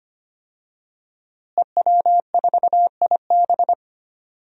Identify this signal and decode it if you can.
Morse code, heard as a keyed tone, about 25 wpm, 700 Hz. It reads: EW4IB
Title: DXCC-100 Call Signs - 5 Characters - Repeated 3x - 25wpm